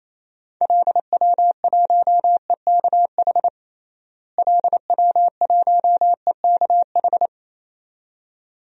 Morse code, heard as a keyed tone, about 28 wpm, 700 Hz. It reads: LW1EKH LW1EKH
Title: DXCC-100 Call Signs - 6 Characters - Repeated 2x - 28wpm